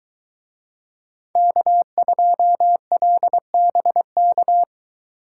Morse code, heard as a keyed tone, about 22 wpm, 700 Hz. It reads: K2LBK